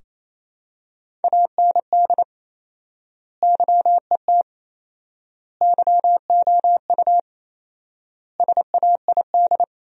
Morse code, heard as a keyed tone, about 28 wpm, 700 Hz. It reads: AND YET YOU SAID
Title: 4x Word-Spacing - Sentences from Top 500 Words - 28wpm